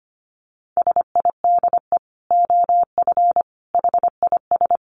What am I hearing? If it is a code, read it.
Morse code, heard as a keyed tone, about 25 wpm, 700 Hz. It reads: SIDE OF HIS
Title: Dit Challenge 25wpm